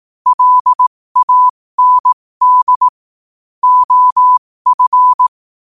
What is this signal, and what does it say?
Morse code, heard as a keyed tone, about 18 wpm, spaced as for 15 wpm, 1000 Hz. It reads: LAND OF
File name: qotdfiles / 15wpm